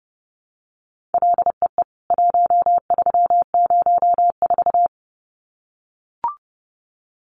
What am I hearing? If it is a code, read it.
Morse code, heard as a keyed tone, about 30 wpm, 700 Hz. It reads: LEE 1304
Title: CWT Exchange 30wpm